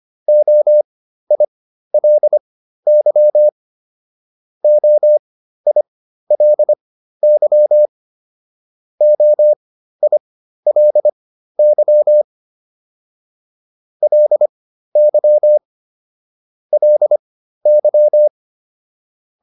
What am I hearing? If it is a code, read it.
Morse code, heard as a keyed tone, about 25 wpm, spaced as for 13 wpm, 600 Hz. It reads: OILY OILY OILY LY LY